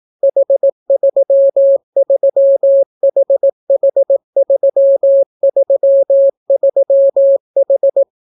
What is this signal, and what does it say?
Morse code, heard as a keyed tone, about 18 wpm, 550 Hz. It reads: H33HH333H